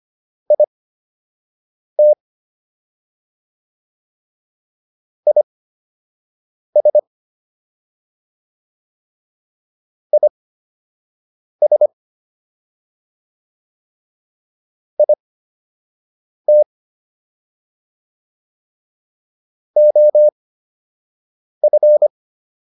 Morse code, heard as a keyed tone, about 25 wpm, spaced as for 6 wpm, 600 Hz. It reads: IT IS IS IT OF